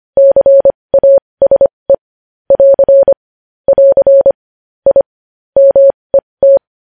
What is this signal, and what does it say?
Morse code, heard as a keyed tone, about 25 wpm, spaced as for 20 wpm, 550 Hz. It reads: CASE <AR> <AR> I MET
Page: news-headlines-in-morse-code-at-20-wpm